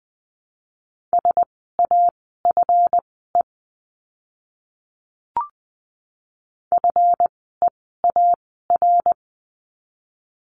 Morse code, heard as a keyed tone, about 20 wpm, spaced as for 14 wpm, 700 Hz. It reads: SAFE FEAR
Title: 139 Lesson 23f - Letter f - Words 20wpm